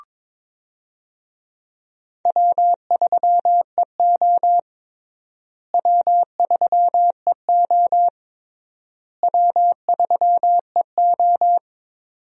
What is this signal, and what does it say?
Morse code, heard as a keyed tone, about 22 wpm, 700 Hz. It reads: W3EO W3EO W3EO
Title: USA Call Signs - 4 Characters - Repeated 3x - 22wpm